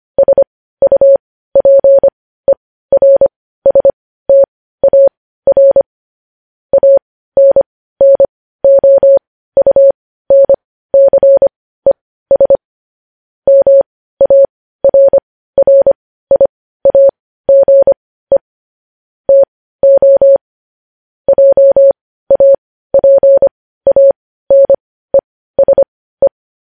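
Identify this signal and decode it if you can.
Morse code, heard as a keyed tone, about 25 wpm, spaced as for 15 wpm, 550 Hz. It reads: SUPERSTAR ANNOUNCES MARRIAGE TO JAPANESE
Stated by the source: Morse code news headlines at 15.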